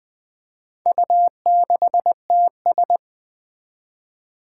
Morse code, heard as a keyed tone, about 20 wpm, 700 Hz. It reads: U6TS
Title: Four Challenging Letter-Number Combinations 20wpm